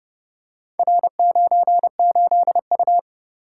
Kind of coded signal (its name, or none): Morse code